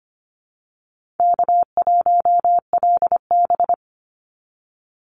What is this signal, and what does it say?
Morse code, heard as a keyed tone, about 25 wpm, 700 Hz. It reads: K1LB
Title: NA SOTA Chaser Callsigns 25wpm